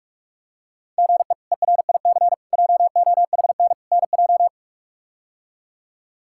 Morse code, heard as a keyed tone, about 45 wpm, 700 Hz. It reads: GE ERIC JOHN NJ
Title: K1USN SST QSO Practice - 45wpm